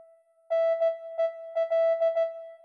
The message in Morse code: NEL